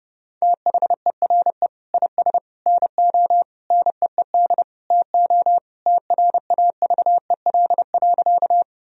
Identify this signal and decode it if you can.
Morse code, heard as a keyed tone, about 30 wpm, 700 Hz. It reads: THERE IS NO NEED TO TRAVEL.
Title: Sentences from Top 1000 Words - Review Part 2 - 30wpm